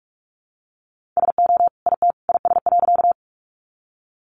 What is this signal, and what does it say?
Morse code, heard as a keyed tone, about 44 wpm, 700 Hz. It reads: SO IT IS.
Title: Sentences from Top 100 Words 45wpm